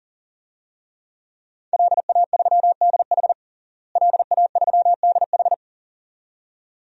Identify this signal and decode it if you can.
Morse code, heard as a keyed tone, about 40 wpm, 700 Hz. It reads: LA3DH LA3DH
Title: DXCC-100 Call Signs - 5 Characters - Repeated 2x - 40wpm